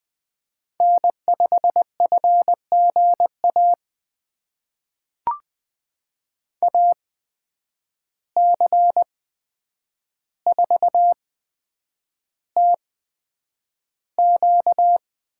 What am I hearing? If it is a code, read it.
Morse code, heard as a keyed tone, about 20 wpm, spaced as for 9 wpm, 700 Hz. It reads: N5FGA AC4TQ
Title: USA Call Signs - 5 Characters - 8x Character Spacing - 20wpm